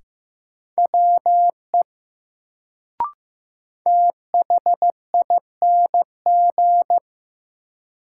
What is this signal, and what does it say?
Morse code, heard as a keyed tone, about 15 wpm, 700 Hz. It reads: WE THING
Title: Sentences from Top 100 Words - New Word Drill - 15wpm